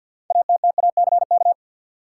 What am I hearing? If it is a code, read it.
Morse code, heard as a keyed tone, about 52 wpm, 700 Hz. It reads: ATTACK
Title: Sentences from Top 1500 Words - New Word Drill - 50wpm